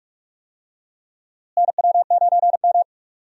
Morse code, heard as a keyed tone, about 43 wpm, 700 Hz. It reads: NW9M
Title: USA Call Signs - 4 Characters - Repeated 2x - 45wpm